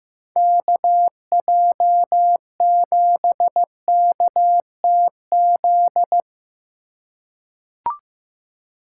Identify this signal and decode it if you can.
Morse code, heard as a keyed tone, about 16 wpm, 700 Hz. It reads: KJ7KTZ
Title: USA Call Signs - 6 Characters - 3x Character Spacing - 15wpm